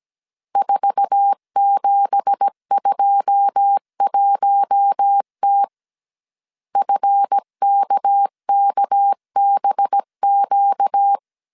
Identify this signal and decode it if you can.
Morse code, heard as a keyed tone, about 17 wpm, 800 Hz. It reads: 4721T FKKBQ
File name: CW / DET / 1 9 1500C S C 16wpm 2909